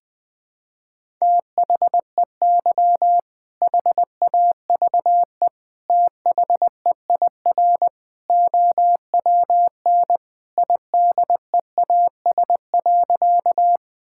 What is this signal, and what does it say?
Morse code, heard as a keyed tone, about 20 wpm, 700 Hz. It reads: THEY HAVE THEIR OWN IDEAS.
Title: Sentences with 2-Word Phrases - 20wpm